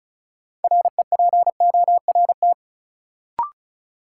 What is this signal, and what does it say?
Morse code, heard as a keyed tone, about 34 wpm, 700 Hz. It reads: REPORT
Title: Sentences from Top 900 Words - New Word Drill - 35wpm